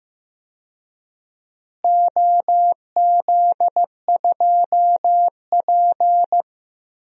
Morse code, heard as a keyed tone, about 15 wpm, 700 Hz. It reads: OZ2P
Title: DXCC-100 Call Signs - 4 Characters - Repeated 3x - 15wpm